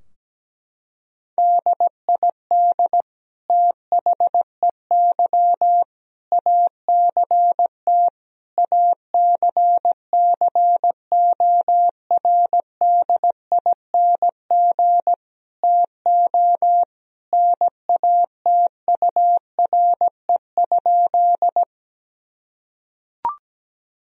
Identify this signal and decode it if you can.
Morse code, heard as a keyed tone, about 17 wpm, 700 Hz. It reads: DID THEY ACT ACCORDING TO NATURE?
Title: Sentences from Top 1000 Words - Review Part 4 - 17wpm